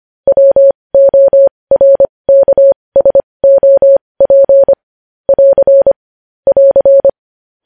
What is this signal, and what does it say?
Morse code, heard as a keyed tone, about 25 wpm, spaced as for 20 wpm, 550 Hz. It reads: WORKSOP <AR> <AR>
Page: news-headlines-in-morse-code-at-20-wpm